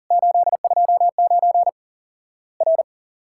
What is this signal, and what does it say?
Morse code, heard as a keyed tone, about 40 wpm, 700 Hz. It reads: 829 R
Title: Worked All Europe DX Contest - QTC Exchange - 40wpm